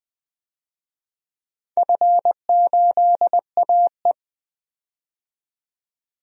Morse code, heard as a keyed tone, about 21 wpm, 700 Hz. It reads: F8AE
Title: DXCC-100 Call Signs - 4 Characters - 20wpm